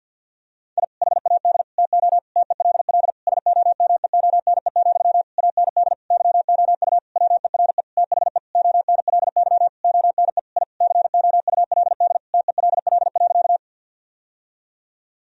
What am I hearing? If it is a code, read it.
Morse code, heard as a keyed tone, about 52 wpm, 700 Hz. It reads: I HAD TO TELL SOMEONE, AND YOU WERE THE ONLY ONE I COULD TELL.
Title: Sentences from Top 1000 Words - Review Part 4 - 50wpm